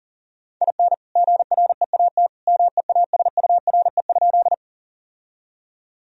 Morse code, heard as a keyed tone, about 40 wpm, 700 Hz. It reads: IN GREAT MEASURE?
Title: Sentences from Top 1000 Words - Review Part 2 - 40wpm